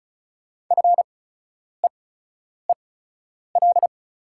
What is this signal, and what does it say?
Morse code, heard as a keyed tone, about 34 wpm, spaced as for 10 wpm, 700 Hz. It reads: FEEL
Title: 8x Character Spacing - Top 500 Words - 35wpm